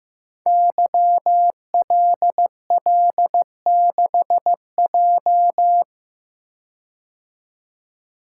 Morse code, heard as a keyed tone, about 15 wpm, 700 Hz. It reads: YLL6J
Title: Code Groups - 5 Letter:Numbers - 15wpm